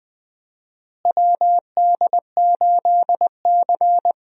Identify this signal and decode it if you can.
Morse code, heard as a keyed tone, about 20 wpm, 700 Hz. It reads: WD8C
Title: USA Call Signs - 4 Characters - Repeated 2x - 20wpm